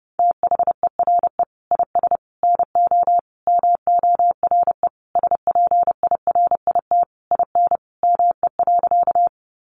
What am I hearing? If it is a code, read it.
Morse code, heard as a keyed tone, about 30 wpm, 700 Hz. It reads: THERE IS NO MORE SPIRIT IN ME.
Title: Sentences from Top 1400 Words 30wpm